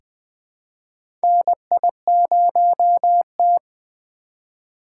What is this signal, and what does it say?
Morse code, heard as a keyed tone, about 20 wpm, 700 Hz. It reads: NI0T
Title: USA Call Signs - 4 Characters - 4x Character Spacing - 20wpm